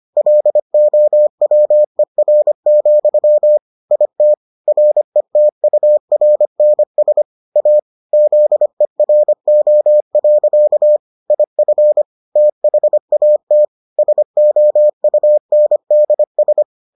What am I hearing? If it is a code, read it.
Morse code, heard as a keyed tone, about 25 wpm, 600 Hz. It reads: LOWER, IT RETURNS A ZERO. IF THAT SOUNDS